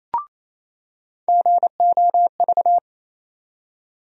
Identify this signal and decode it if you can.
Morse code, heard as a keyed tone, about 27 wpm, 700 Hz. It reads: GOV
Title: USA Call Sign Postfixes - 3 Letters 28wpm